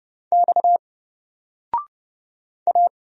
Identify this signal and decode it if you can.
Morse code, heard as a keyed tone, about 31 wpm, 700 Hz. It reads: X A
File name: Single-Letter-Number-30wpm